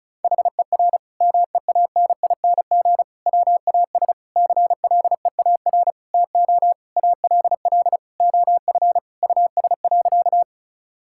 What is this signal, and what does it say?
Morse code, heard as a keyed tone, about 36 wpm, 700 Hz. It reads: HER MEANING WAS CLEAR TO ALL OF US.
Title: Sentences from Top 1000 Words - Review Part 6 - 35wpm